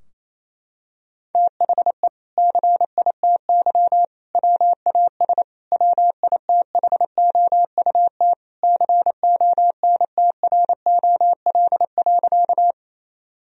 Morse code, heard as a keyed tone, about 27 wpm, 700 Hz. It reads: THE CITY WAS WITHOUT CONTROL.
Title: Sentences from Top 1000 Words - Review Part 3 - 28wpm